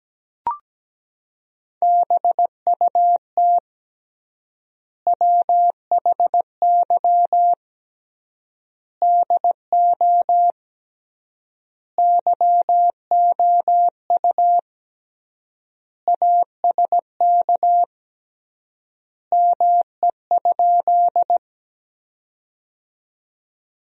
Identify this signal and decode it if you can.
Morse code, heard as a keyed tone, about 17 wpm, 700 Hz. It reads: BUT WHY DO YOU ASK ME?
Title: 3x Word-Spacing - Sentences from Top 200 Words - 17wpm